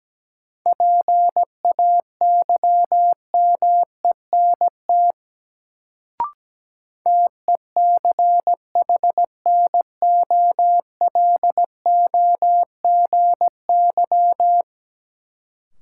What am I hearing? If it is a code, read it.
Morse code, heard as a keyed tone, about 17 wpm, 700 Hz. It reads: PAYMENT TECHNOLOGY
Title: Sentences from Top 1200 Words - New Word Drill Alt - 17wpm